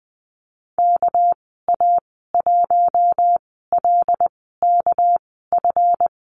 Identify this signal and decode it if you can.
Morse code, heard as a keyed tone, about 20 wpm, spaced as for 14 wpm, 700 Hz. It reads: KA1LKF
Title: 245 Lesson 39g - Letter z - Call Signs 20wpm